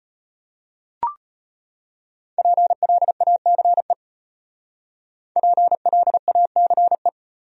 Morse code, heard as a keyed tone, about 35 wpm, 700 Hz. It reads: PLACE PLACE